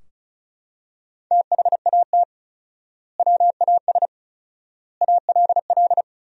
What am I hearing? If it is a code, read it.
Morse code, heard as a keyed tone, about 34 wpm, 700 Hz. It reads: THAT WAS ALL
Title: 4x Word-Spacing - Sentences from Top 100 Words - 35wpm